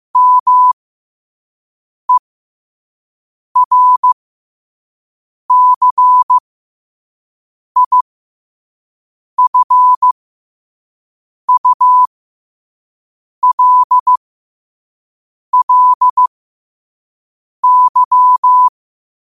Morse code, heard as a keyed tone, about 15 wpm, spaced as for 5 wpm, 1000 Hz. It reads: MERCIFULLY
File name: qotdfiles / 5wpm